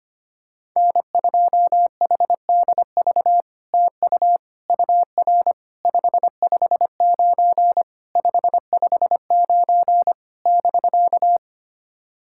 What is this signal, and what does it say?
Morse code, heard as a keyed tone, about 25 wpm, 700 Hz. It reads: N2HDV TU UR 559 559 <BK>
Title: SOTA QSO Practice 25wpm